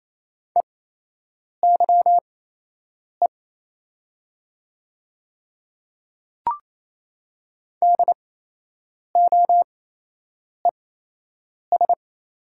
Morse code, heard as a keyed tone, about 28 wpm, spaced as for 8 wpm, 690 Hz. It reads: EYE DOES
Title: 8x Character Spacing - Top 300 Words - 28wpm